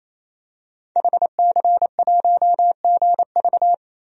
Morse code, heard as a keyed tone, about 28 wpm, 700 Hz. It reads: HC1GV
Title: DXCC-100 Call Signs - 5 Characters - 28wpm